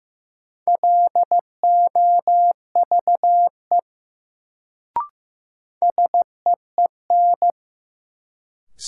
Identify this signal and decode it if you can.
Morse code, heard as a keyed tone, about 15 wpm, 700 Hz. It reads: LOVE SEEN
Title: Sets of 5 Words Encore - New Word Drill - 15wpm